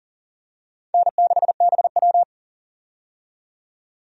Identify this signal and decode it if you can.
Morse code, heard as a keyed tone, about 39 wpm, 700 Hz. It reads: N6BW
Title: USA Call Signs - 4 Characters - 40wpm